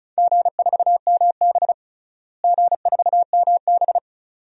Morse code, heard as a keyed tone, about 36 wpm, 700 Hz. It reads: G4MB G4MB
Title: DXCC-100 Call Signs - 4 Characters - Repeated 2x - 35wpm